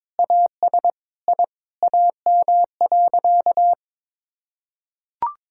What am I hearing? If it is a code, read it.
Morse code, heard as a keyed tone, about 22 wpm, 700 Hz. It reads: AS I AM.
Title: Sentences from Top 1000 Words - Review Part 6 - 22wpm